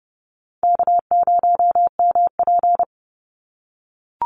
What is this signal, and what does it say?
Morse code, heard as a keyed tone, about 30 wpm, 700 Hz. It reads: K0MP
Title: NA SOTA Chaser Callsigns 30wpm